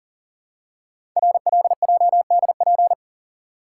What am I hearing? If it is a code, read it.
Morse code, heard as a keyed tone, about 38 wpm, 690 Hz. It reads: RLJDP
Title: Code Groups - 5 Letters - 40wpm